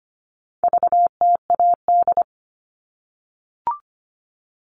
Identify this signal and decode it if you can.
Morse code, heard as a keyed tone, about 25 wpm, 700 Hz. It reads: VTAD
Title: All Cut Numbers - 4-Digit 25wpm